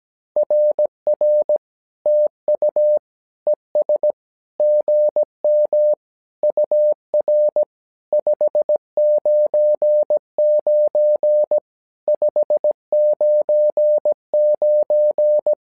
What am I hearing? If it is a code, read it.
Morse code, heard as a keyed tone, about 17 wpm, 600 Hz. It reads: RR TU ES GM UR 599 599
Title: SOTA QSO Practice 17wpm